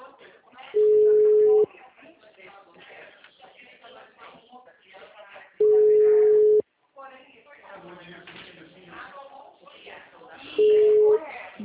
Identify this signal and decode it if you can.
Ringback tone (Europe, 425 Hz), with speech.